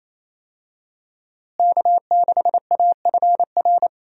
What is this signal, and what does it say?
Morse code, heard as a keyed tone, about 28 wpm, 700 Hz. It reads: K6AFR
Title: USA Call Signs - 5 Characters - Repeated 3x - 28wpm